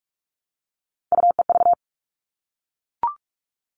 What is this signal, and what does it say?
Morse code, heard as a keyed tone, about 44 wpm, 700 Hz. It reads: UEV
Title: All Cut Numbers - 3-Digit 45wpm